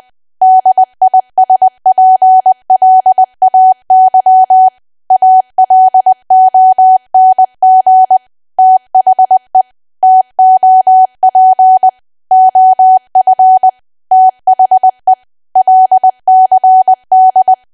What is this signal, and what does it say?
Morse code, heard as a keyed tone, about 20 wpm, 750 Hz. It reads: DISPLAY ALONG THE TOP OF THE LCD
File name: cwa / W1AW-Level3 / 20WPM